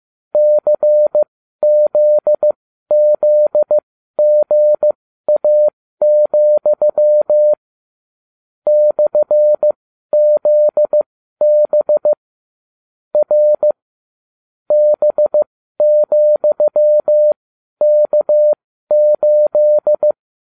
Morse code, heard as a keyed tone, about 15 wpm, spaced as for 12 wpm, 600 Hz. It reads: CZZGA, /ZB R B,K8